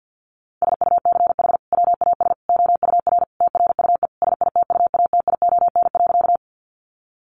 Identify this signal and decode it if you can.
Morse code, heard as a keyed tone, about 50 wpm, 700 Hz. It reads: SUCH WAS OUR TRUE SITUATION.
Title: Sentences from Top 1200 Words 50wpm